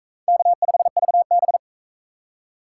Morse code, heard as a keyed tone, about 43 wpm, 700 Hz. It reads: K5VB